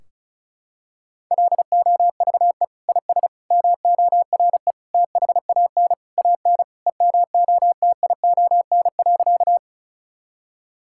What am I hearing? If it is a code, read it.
Morse code, heard as a keyed tone, about 36 wpm, 700 Hz. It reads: LOVE IS MORE THAN AN EMOTION.
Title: Sentences from Top 1600 Words 35wpm